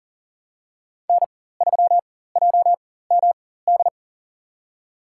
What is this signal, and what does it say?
Morse code, heard as a keyed tone, about 40 wpm, spaced as for 19 wpm, 700 Hz. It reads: N3JMD
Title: USA Call Signs - 5 Characters - 4x Character Spacing - 40wpm